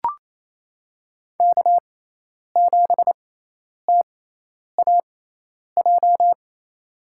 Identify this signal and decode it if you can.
Morse code, heard as a keyed tone, about 28 wpm, spaced as for 10 wpm, 700 Hz. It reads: K7TAJ
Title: USA Call Signs - 5 Characters - 6x Character Spacing - 28wpm